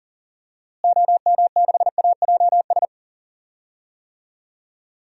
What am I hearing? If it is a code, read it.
Morse code, heard as a keyed tone, about 40 wpm, 700 Hz. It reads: OM6AJS
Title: DXCC-100 Call Signs - 6 Characters - Repeated 2x - 40wpm